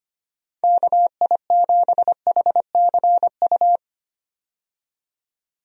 Morse code, heard as a keyed tone, about 25 wpm, 700 Hz. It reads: KI7HCU
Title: USA Call Signs - 6 Characters - 25wpm